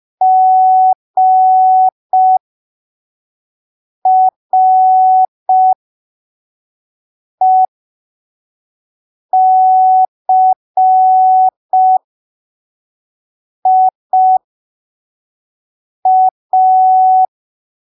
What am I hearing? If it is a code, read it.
Morse code, heard as a keyed tone, about 5 wpm, spaced as for 3 wpm, 750 Hz. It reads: GRECIA
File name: arquivos / texto / texto 08 05ppm